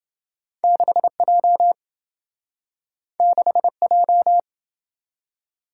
Morse code, heard as a keyed tone, about 27 wpm, 700 Hz. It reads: 6J 6J